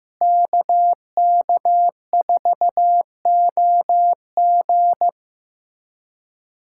Morse code, heard as a keyed tone, about 15 wpm, 700 Hz. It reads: KK4OG
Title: USA Callsign DE Callsign - 5 Characters - 15wpm